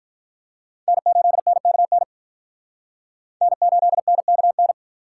Morse code, heard as a keyed tone, about 43 wpm, 700 Hz. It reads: N8NKN N8NKN
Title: Speed Racing - USA 1x3 Call Signs 35wpm